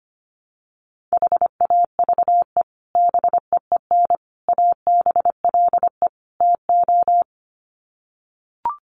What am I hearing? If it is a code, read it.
Morse code, heard as a keyed tone, about 25 wpm, 700 Hz. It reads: HAVE BEEN ABLE TO
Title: Sets of 4 Words 25wpm